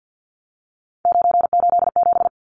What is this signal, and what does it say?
Morse code, heard as a keyed tone, about 48 wpm, 700 Hz. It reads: 987
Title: Numbers - 3-Digit 50wpm